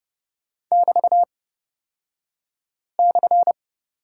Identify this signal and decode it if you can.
Morse code, heard as a keyed tone, about 30 wpm, spaced as for 5 wpm, 700 Hz. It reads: <BT>/
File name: Lesson 36d - BT - Two Characters 30wpm